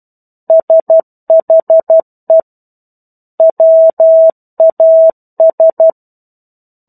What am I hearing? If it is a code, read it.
Morse code, heard as a keyed tone, about 12 wpm, 650 Hz. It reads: SHE WAS
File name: 01-Chapter-01 12 WPM